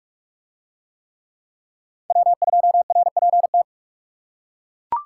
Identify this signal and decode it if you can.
Morse code, heard as a keyed tone, about 42 wpm, 700 Hz. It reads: W2RPT